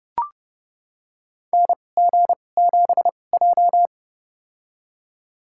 Morse code, heard as a keyed tone, about 32 wpm, spaced as for 22 wpm, 700 Hz. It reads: NG7J
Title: USA Call Signs - 2x Character Spacing - 30wpm